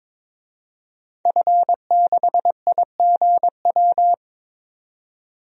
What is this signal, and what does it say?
Morse code, heard as a keyed tone, about 22 wpm, 700 Hz. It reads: F6IGW
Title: DXCC-100 Call Signs - 5 Characters - 22wpm